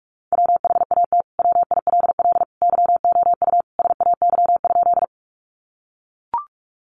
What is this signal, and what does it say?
Morse code, heard as a keyed tone, about 44 wpm, 700 Hz. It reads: WHAT WILL YOU SAY?
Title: Sentences from Top 200 Words 45wpm